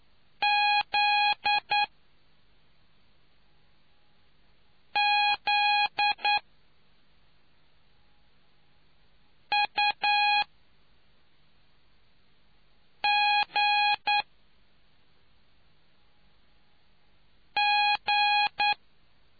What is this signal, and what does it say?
Morse code, heard as a keyed tone, about 9 wpm, spaced as for 3 wpm, 830 Hz. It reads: ZZUGG